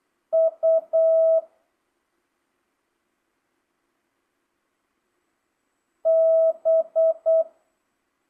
Morse code, heard as a keyed tone, about 8 wpm, 650 Hz. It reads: U B